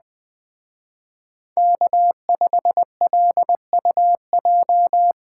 Morse code, heard as a keyed tone, about 19 wpm, 700 Hz. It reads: K5LUJ
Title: Speed Racing - USA Call Signs 17wpm